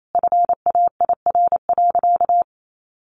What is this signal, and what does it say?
Morse code, heard as a keyed tone, about 28 wpm, 700 Hz. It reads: FAIR.
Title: Sentences from Top 1300 Words 28wpm